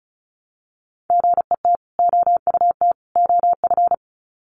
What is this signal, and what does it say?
Morse code, heard as a keyed tone, about 34 wpm, 700 Hz. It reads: GET OUT OF
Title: Sets of 3 Words 35wpm